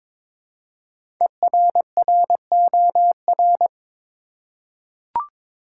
Morse code, heard as a keyed tone, about 22 wpm, 700 Hz. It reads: ERROR
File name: 5 Letter Words - First 3 Letters Given - 22wpm